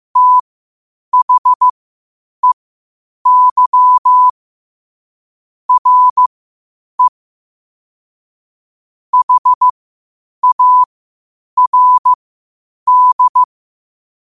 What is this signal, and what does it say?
Morse code, heard as a keyed tone, about 15 wpm, spaced as for 8 wpm, 1000 Hz. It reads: THEY RE HARD
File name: qotdfiles / 8wpm